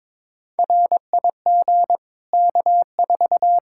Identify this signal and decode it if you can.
Morse code, heard as a keyed tone, about 22 wpm, 700 Hz. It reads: RIG K4
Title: Practice QSO - Exchange 2 - Info - 22wpm